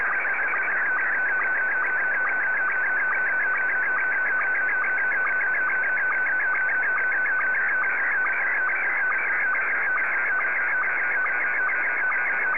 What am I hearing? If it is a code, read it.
SSTV (slow-scan television), Scottie 1.
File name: sstv / S1-VK3DEK